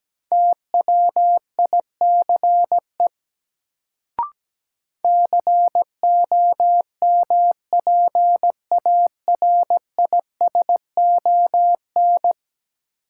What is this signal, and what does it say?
Morse code, heard as a keyed tone, about 17 wpm, 700 Hz. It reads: TWICE COMPARISON
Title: Sentences from Top 1600 Words - New Word Drill - 17wpm